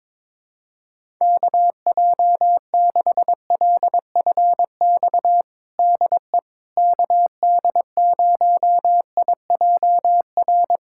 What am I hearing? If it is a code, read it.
Morse code, heard as a keyed tone, about 22 wpm, 700 Hz. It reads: KJ6LFX DE KD0IJR
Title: USA Callsign DE Callsign - 6 Characters - 22wpm